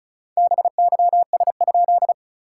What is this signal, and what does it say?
Morse code, heard as a keyed tone, about 36 wpm, 660 Hz. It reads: BYS?
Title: SOTA QSO Practice 35wpm